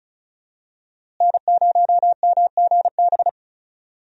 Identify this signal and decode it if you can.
Morse code, heard as a keyed tone, about 34 wpm, 700 Hz. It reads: N0MGB
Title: USA Call Signs - 5 Characters - Repeated 3x - 35wpm